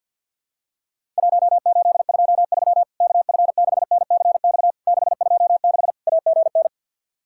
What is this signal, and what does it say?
Morse code, heard as a keyed tone, about 48 wpm, 700 Hz. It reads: 1823 KF6NCX 626 AGN
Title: Worked All Europe DX Contest - QTC Exchange with Repeats - 50wpm